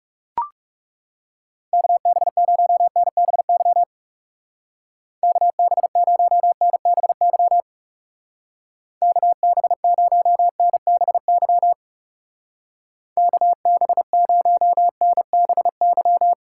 Morse code, heard as a keyed tone, about 40 wpm, 700 Hz. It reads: KB0NBY KB0NBY KB0NBY KB0NBY